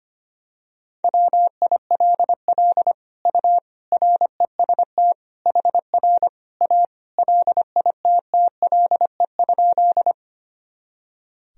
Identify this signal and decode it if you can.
Morse code, heard as a keyed tone, about 25 wpm, 700 Hz. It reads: WILL U REST HR A LITTLE?
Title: Sentences from Top 500 Words and Abbreviations 25wpm